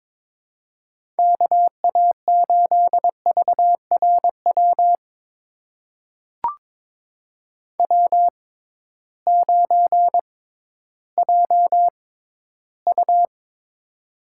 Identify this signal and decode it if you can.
Morse code, heard as a keyed tone, about 22 wpm, spaced as for 13 wpm, 700 Hz. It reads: KA8VRW W9JU
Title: USA Call Signs - 6x Character Spacing - 22wpm